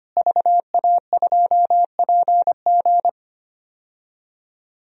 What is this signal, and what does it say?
Morse code, heard as a keyed tone, about 25 wpm, 700 Hz. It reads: VA2PG
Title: DXCC-100 Call Signs - 5 Characters - Repeated 3x - 25wpm